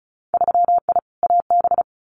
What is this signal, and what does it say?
Morse code, heard as a keyed tone, about 36 wpm, 700 Hz. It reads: 3I AB
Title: Winter Field Day Exchanges Encore 35wpm